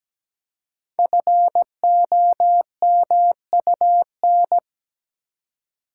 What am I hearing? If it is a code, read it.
Morse code, heard as a keyed tone, about 17 wpm, 700 Hz. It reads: FOMUN
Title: Code Groups - 5 Letters - 17wpm